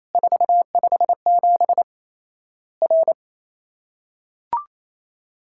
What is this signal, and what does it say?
Morse code, heard as a keyed tone, about 28 wpm, 700 Hz. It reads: 457 R
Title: Worked All Europe DX Contest - QTC Exchange - 28wpm